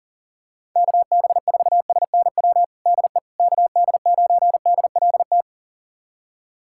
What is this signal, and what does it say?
Morse code, heard as a keyed tone, about 40 wpm, 700 Hz. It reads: KB4SNW DE KD9DLT
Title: USA Callsign DE Callsign 40wpm